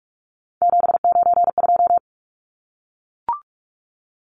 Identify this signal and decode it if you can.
Morse code, heard as a keyed tone, about 44 wpm, 700 Hz. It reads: 792
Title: Numbers - 3-Digit 45wpm